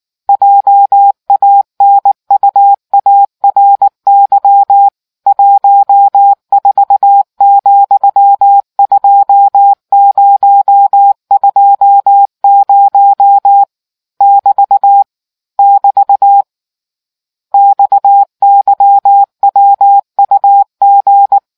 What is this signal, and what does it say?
Morse code, heard as a keyed tone, about 19 wpm, 800 Hz. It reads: JANUARY 14,2020 <BT> <BT> XYWUG